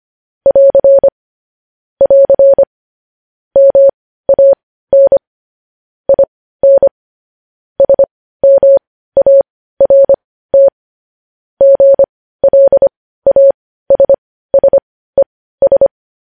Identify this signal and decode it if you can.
Morse code, heard as a keyed tone, about 25 wpm, spaced as for 15 wpm, 550 Hz. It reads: <AR> <AR> MAN IN SMART GLASSES
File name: podcast / News headlines-1x-25x15